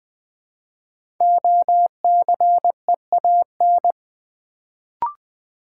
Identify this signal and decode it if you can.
Morse code, heard as a keyed tone, about 20 wpm, 700 Hz. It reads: OCEAN
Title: Build A Word - 1 Letter At A Time - 5-Letter Words - 20wpm